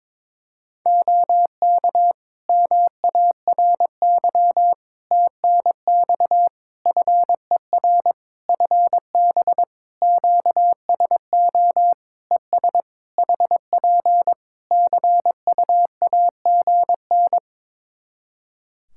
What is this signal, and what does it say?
Morse code, heard as a keyed tone, about 22 wpm, 700 Hz. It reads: OK MARY TNX FER FB QSO ES HP CUAGN
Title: Practice QSO - Exchange 4 - Ending - 22wpm